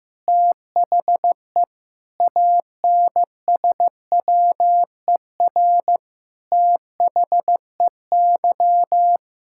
Morse code, heard as a keyed tone, about 16 wpm, 700 Hz. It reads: THE ANSWER THEY
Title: Sentences from Top 1000 Words - Review Part 6 - 15wpm